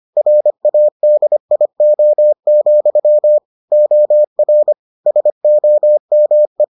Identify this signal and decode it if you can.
Morse code, heard as a keyed tone, about 25 wpm, 600 Hz. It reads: RADIO, OR SOME